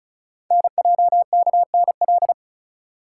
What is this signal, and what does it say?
Morse code, heard as a keyed tone, about 36 wpm, 690 Hz. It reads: NJKNL
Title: Code Groups - 5 Letter:Numbers - 35wpm